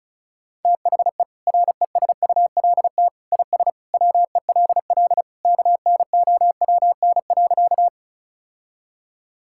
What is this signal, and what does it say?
Morse code, heard as a keyed tone, about 36 wpm, 700 Hz. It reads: THE RESULT IS WELL KNOWN.
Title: Sentences from Top 1000 Words - Review Part 5 - 35wpm